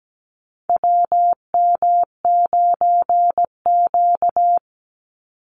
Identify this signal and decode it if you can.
Morse code, heard as a keyed tone, about 17 wpm, 700 Hz. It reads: WM9Q
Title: Call Signs Repeated 3x - 17wpm